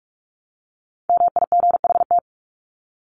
Morse code, heard as a keyed tone, about 44 wpm, 700 Hz. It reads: MIGHT
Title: Top 300 Words 45wpm